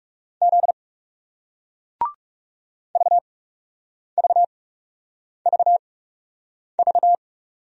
Morse code, heard as a keyed tone, about 40 wpm, spaced as for 8 wpm, 700 Hz. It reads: Z VVVV